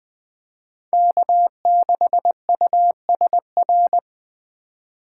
Morse code, heard as a keyed tone, about 20 wpm, 700 Hz. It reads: K6USR
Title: Speed Racing - USA 1x3 Call Signs 17wpm